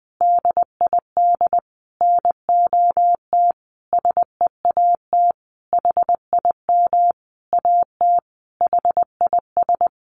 Morse code, heard as a keyed tone, about 20 wpm, 700 Hz. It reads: DID NOT SEAT HIM AT HIS
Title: Sentences from Top 1500 Words 20wpm